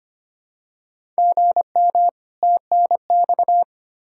Morse code, heard as a keyed tone, about 25 wpm, 700 Hz. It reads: GM TNX
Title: Practice QSO - Exchange 1 - Beginning - 25wpm